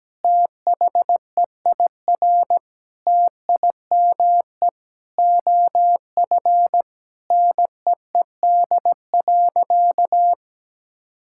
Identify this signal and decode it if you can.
Morse code, heard as a keyed tone, about 17 wpm, 700 Hz. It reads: THEIR TIME OF NEED.
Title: Sentences from Top 1000 Words - Review Part 5 - 17wpm